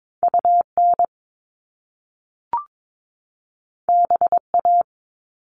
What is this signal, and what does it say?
Morse code, heard as a keyed tone, about 22 wpm, 700 Hz. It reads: UN BA
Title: Top Two Letter Combinations in Words 22wpm